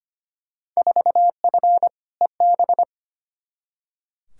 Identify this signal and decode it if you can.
Morse code, heard as a keyed tone, about 25 wpm, 700 Hz. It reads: 4F EB
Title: Field Day Exchanges - 25wpm